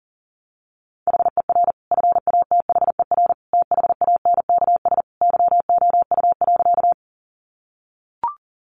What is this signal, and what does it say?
Morse code, heard as a keyed tone, about 40 wpm, 700 Hz. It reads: HER FATHER THANKS YOU.
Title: Sentences from Top 600 Words 40wpm